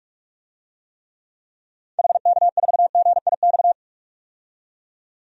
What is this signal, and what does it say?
Morse code, heard as a keyed tone, about 46 wpm, 700 Hz. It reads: HK4GIX
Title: DXCC-100 Call Signs - 6 Characters - 45wpm